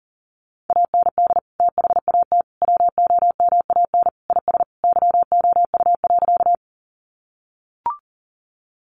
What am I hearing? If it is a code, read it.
Morse code, heard as a keyed tone, about 40 wpm, 700 Hz. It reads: AND THAT WOMAN IS YOU.
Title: Sentences from Top 1100 Words 40wpm